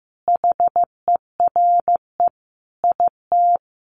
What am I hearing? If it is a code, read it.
Morse code, heard as a keyed tone, about 14 wpm, 700 Hz. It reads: HERE IT
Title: Sentences from Top 1400 Words 15wpm